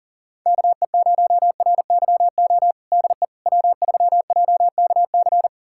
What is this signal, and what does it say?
Morse code, heard as a keyed tone, about 40 wpm, 700 Hz. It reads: KE0RYO DE W3JKC
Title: USA Callsign DE Callsign 40wpm